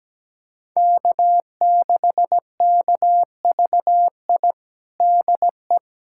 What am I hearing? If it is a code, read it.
Morse code, heard as a keyed tone, about 17 wpm, 700 Hz. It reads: K6KVI DE
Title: USA Callsign DE Callsign - 5 Characters - 17wpm